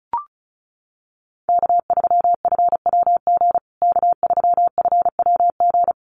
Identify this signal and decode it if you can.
Morse code, heard as a keyed tone, about 34 wpm, 700 Hz. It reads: K3FWG K3FWG
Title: USA Call Signs Repeated 3x 35wpm